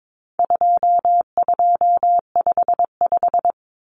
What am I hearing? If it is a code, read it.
Morse code, heard as a keyed tone, about 22 wpm, 700 Hz. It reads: 2255
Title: Numbers - 4-Digit 22wpm